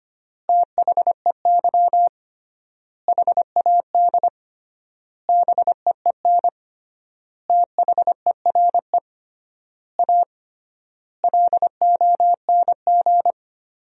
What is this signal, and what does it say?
Morse code, heard as a keyed tone, about 25 wpm, 700 Hz. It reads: THEY HAD BEEN THERE A LONG
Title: 3x Word-Spacing - Sentences from Top 400 Words - 25wpm